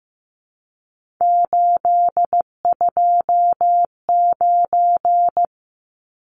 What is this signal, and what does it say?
Morse code, heard as a keyed tone, about 15 wpm, 700 Hz. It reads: 829
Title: Numbers - 3-Digit 15wpm